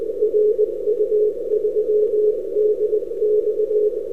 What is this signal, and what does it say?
Morse code, heard as a keyed tone, about 20 wpm, 420 Hz. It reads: RU3DX
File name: n6tr / ru3dx 11-Oct-2008